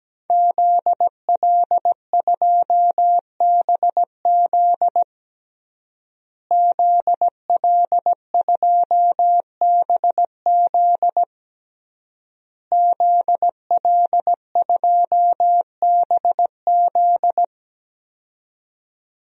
Morse code, heard as a keyed tone, about 17 wpm, 700 Hz. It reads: ZL2BZ ZL2BZ ZL2BZ
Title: DXCC-100 Call Signs - 5 Characters - Repeated 3x - 17wpm